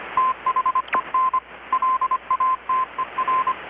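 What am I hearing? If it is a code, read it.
Morse code, heard as a keyed tone, about 25 wpm, 1040 Hz, with radio noise. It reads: THEN LATER
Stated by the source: Morse from KPH's Night of Nights XX maritime radiotelegraphy event, received in Chiapas, Mexico about 3,800 km away on a Kenwood R-600 with a grounded 10 m long-wire antenna.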